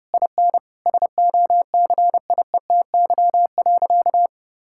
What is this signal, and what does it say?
Morse code, heard as a keyed tone, about 32 wpm, 700 Hz. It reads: IN SOCIETY.
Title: Sentences from Top 1600 Words 30wpm